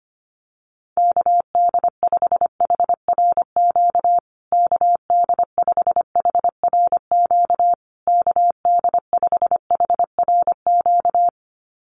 Morse code, heard as a keyed tone, about 25 wpm, 700 Hz. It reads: KD5HRQ KD5HRQ KD5HRQ
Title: USA Call Signs Repeated 3x 25wpm